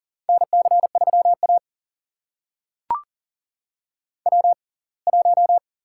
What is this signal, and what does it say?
Morse code, heard as a keyed tone, about 40 wpm, spaced as for 27 wpm, 700 Hz. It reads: NC3A W1
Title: USA Call Signs - 4 Characters - 6x Character Spacing - 40wpm